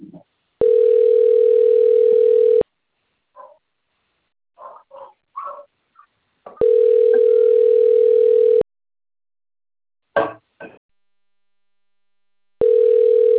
Ringback tone (North America), 440 + 480 Hz.